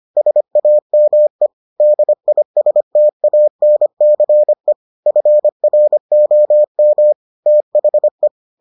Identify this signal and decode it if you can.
Morse code, heard as a keyed tone, about 25 wpm, 600 Hz. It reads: SAME DISTANCE FROM THE